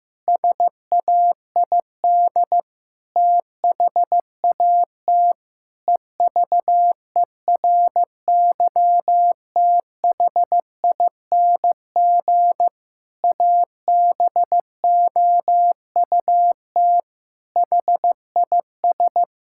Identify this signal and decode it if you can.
Morse code, heard as a keyed tone, about 15 wpm, 700 Hz. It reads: SAID THAT EVERYTHING ABOUT HIS
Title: Pangram Challenge v2 - 15wpm